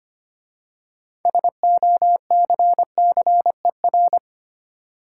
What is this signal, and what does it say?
Morse code, heard as a keyed tone, about 25 wpm, 700 Hz. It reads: SOCCER
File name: practice / 25wpm / Easy Words with Hints - 25wpm